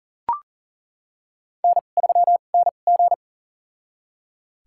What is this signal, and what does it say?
Morse code, heard as a keyed tone, about 37 wpm, spaced as for 28 wpm, 700 Hz. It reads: N3NG
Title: USA Call Signs - 2x Character Spacing - 40wpm